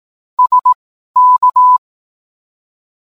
Morse code, heard as a keyed tone, about 18 wpm, spaced as for 12 wpm, 1000 Hz. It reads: SK